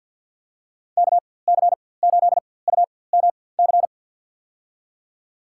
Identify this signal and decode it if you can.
Morse code, heard as a keyed tone, about 52 wpm, spaced as for 23 wpm, 700 Hz. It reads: KC8UMC
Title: USA Call Signs - 6 Characters - 4x Character Spacing - 50wpm